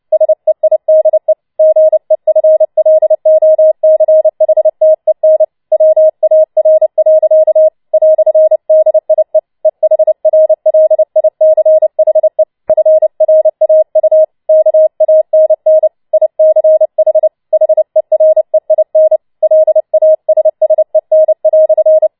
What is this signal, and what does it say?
Morse code, heard as a keyed tone, about 29 wpm, 630 Hz. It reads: SEIDE GEFLOCHTEN WAR. "DIE EHRLICHE FRAU KANN ICH HEREIN LASSEN"
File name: Schneewittchen Teil3 30wpm